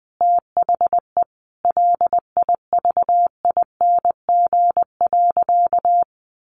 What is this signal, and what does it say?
Morse code, heard as a keyed tone, about 20 wpm, 700 Hz. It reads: THE LIVING.
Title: Sentences from Top 1100 Words 20wpm